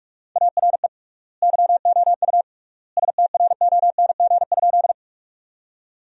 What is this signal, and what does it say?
Morse code, heard as a keyed tone, about 46 wpm, 700 Hz. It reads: ARE YOU STRONG?
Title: 3x Word-Spacing - Sentences from Top 500 Words - 45wpm